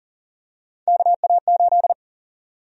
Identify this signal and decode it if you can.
Morse code, heard as a keyed tone, about 38 wpm, 700 Hz. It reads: KA8
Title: USA Call Sign Prefixes - 40wpm